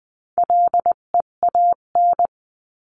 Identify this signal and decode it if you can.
Morse code, heard as a keyed tone, about 20 wpm, 700 Hz. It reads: LEAN
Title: 258 Lesson 42c - Words 16-20wpm